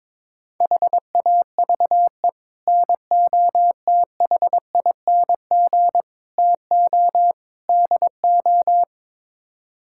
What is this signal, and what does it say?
Morse code, heard as a keyed tone, about 22 wpm, 700 Hz. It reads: HAVE NOTHING TO DO
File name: practice / 22wpm / Build A Phrase - 5-Word Phrases - 22wpm